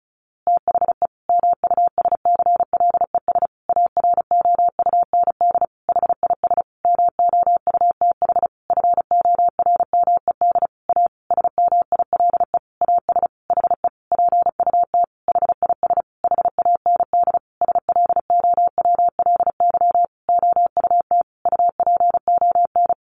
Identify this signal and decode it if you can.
Morse code, heard as a keyed tone, about 34 wpm, 700 Hz. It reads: THE MUSCLES AROUND HIS MOUTH FORMED A SMILE AS HE PUT HIS HAND SLOWLY OUT UPON
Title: A Sound Of Thunder 35wpm